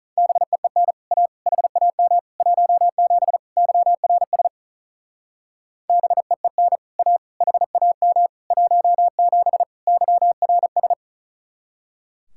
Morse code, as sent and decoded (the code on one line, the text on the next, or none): -... . -. / .- / .... .- -- / .---- --... / -.-- .-. ... / -... . -. / .- / .... .- -- / .---- --... / -.-- .-. ...
BEEN A HAM 17 YRS BEEN A HAM 17 YRS